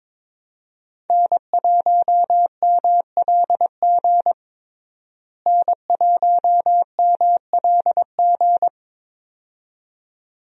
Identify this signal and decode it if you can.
Morse code, heard as a keyed tone, about 22 wpm, 700 Hz. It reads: N1MLG N1MLG
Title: USA Call Signs - 5 Characters - Repeated 2x - 22wpm